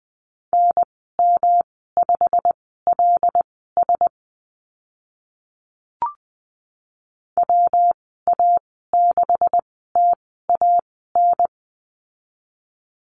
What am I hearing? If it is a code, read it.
Morse code, heard as a keyed tone, about 20 wpm, spaced as for 14 wpm, 700 Hz. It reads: NM5LS WA6TAN
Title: 128 Lesson 21g - Number 6 - Call Signs 20wpm